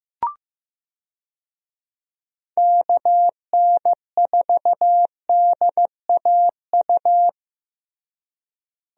Morse code, heard as a keyed tone, about 16 wpm, 700 Hz. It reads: KN4DAU
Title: USA Call Signs - 6 Characters - Repeated 2x - 15wpm